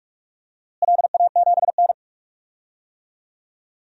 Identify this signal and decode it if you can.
Morse code, heard as a keyed tone, about 46 wpm, 690 Hz. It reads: LA7N
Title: DXCC-100 Call Signs - 4 Characters - Repeated 3x - 45wpm